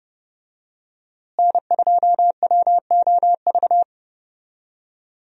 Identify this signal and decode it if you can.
Morse code, heard as a keyed tone, about 30 wpm, 700 Hz. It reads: N2WOV
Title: USA Call Signs - 5 Characters - Repeated 2x - 30wpm